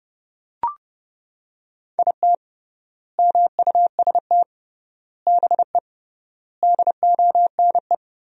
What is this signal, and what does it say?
Morse code, heard as a keyed tone, about 30 wpm, 700 Hz. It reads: IT MUST BE DONE
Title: 3x Word-Spacing - Sentences from Top 500 Words - 30wpm